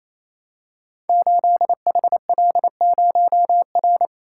Morse code, heard as a keyed tone, about 27 wpm, 700 Hz. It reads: 8HL0R
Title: Code Groups - 5 Letter:Numbers - 28wpm